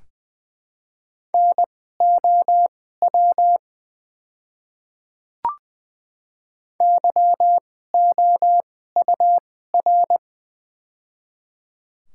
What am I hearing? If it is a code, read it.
Morse code, heard as a keyed tone, about 20 wpm, spaced as for 14 wpm, 700 Hz. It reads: NOW YOUR
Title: 2x Character Spacing - Top 100 Words - 20wpm